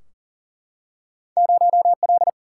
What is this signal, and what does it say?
Morse code, heard as a keyed tone, about 40 wpm, 700 Hz. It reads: 0 L
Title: Single-Letter Single-Number 40wpm